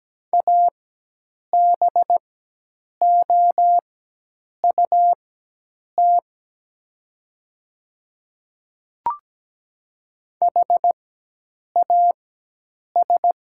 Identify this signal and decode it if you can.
Morse code, heard as a keyed tone, about 17 wpm, spaced as for 8 wpm, 700 Hz. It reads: ABOUT HAS
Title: 4x Character Spacing - Top 100 Words - 17wpm